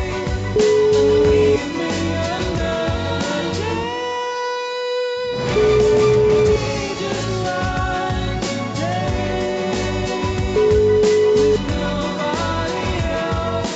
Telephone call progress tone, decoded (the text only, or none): ringback tone